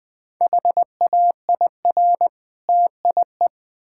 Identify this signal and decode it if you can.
Morse code, heard as a keyed tone, about 20 wpm, 690 Hz. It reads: HAIR TIE
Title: Difficult Words with Hints - 20wpm